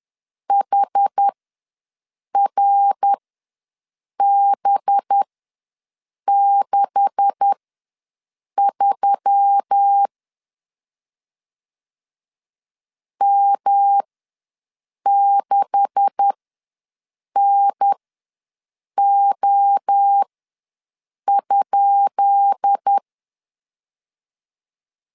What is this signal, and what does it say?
Morse code, heard as a keyed tone, about 10 wpm, spaced as for 6 wpm, 800 Hz. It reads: HRB63 M6NO?
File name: LEZIONI / 10 SIGNS / 10D COMA ws 10wpm 4314